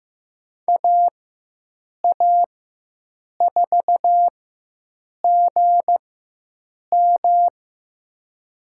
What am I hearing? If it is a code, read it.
Morse code, heard as a keyed tone, about 15 wpm, spaced as for 7 wpm, 700 Hz. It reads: AA4GM
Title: USA Call Signs - 5 Characters - 4x Character Spacing - 15wpm